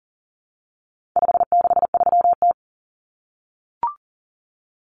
Morse code, heard as a keyed tone, about 40 wpm, 700 Hz. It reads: 563T
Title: Common Cut Numbers - 4-Digit 40wpm